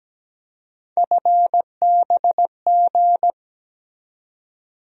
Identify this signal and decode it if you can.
Morse code, heard as a keyed tone, about 17 wpm, 700 Hz. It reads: FBG
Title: USA Call Sign Postfixes - 3 Letters 17wpm